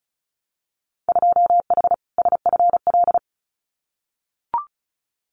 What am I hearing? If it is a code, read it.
Morse code, heard as a keyed tone, about 36 wpm, 700 Hz. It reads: 2H SFL
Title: Winter Field Day Exchanges Encore 35wpm